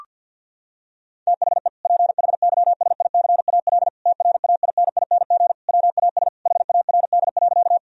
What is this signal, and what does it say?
Morse code, heard as a keyed tone, about 48 wpm, 700 Hz. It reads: THE PHYSICAL TRAINING WAS HARD.
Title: Sentences from Top 1000 Words - Review Part 4 - 50wpm